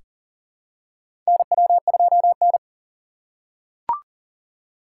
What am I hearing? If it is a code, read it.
Morse code, heard as a keyed tone, about 40 wpm, 700 Hz. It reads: NW2N
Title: USA Call Signs - 3x Character Spacing - 40wpm